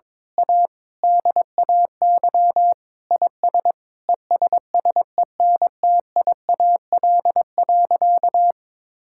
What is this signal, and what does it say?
Morse code, heard as a keyed tone, about 22 wpm, 700 Hz. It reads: A DAY IS ESSENTIAL.
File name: Difficult Sentences with Hints - 22wpm